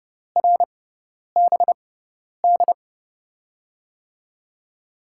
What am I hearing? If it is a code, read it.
Morse code, heard as a keyed tone, about 30 wpm, spaced as for 10 wpm, 700 Hz. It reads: RBD
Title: USA Call Signs - 5 Characters - 6x Character Spacing - 30wpm